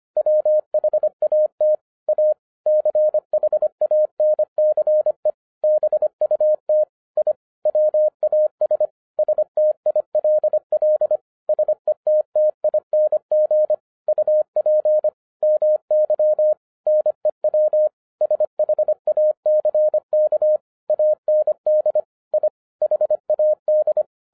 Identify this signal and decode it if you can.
Morse code, heard as a keyed tone, about 25 wpm, 610 Hz. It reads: WHAT A CHANCE BUT I WAS STILL SETTING UP MY NEW SHACK AND I HAD
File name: FIRST FISTS CONTACT-25wpm